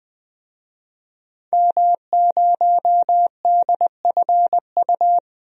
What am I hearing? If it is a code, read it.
Morse code, heard as a keyed tone, about 20 wpm, 700 Hz. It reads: M0DFU
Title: DXCC-100 Call Signs - 5 Characters - 20wpm